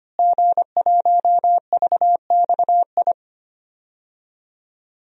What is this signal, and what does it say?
Morse code, heard as a keyed tone, about 25 wpm, 700 Hz. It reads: G1VXI
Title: DXCC-100 Call Signs - 5 Characters - Repeated 3x - 25wpm